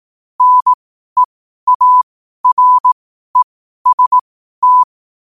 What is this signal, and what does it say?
Morse code, heard as a keyed tone, about 18 wpm, spaced as for 12 wpm, 1000 Hz. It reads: NEAREST
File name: qotdfiles / 12wpm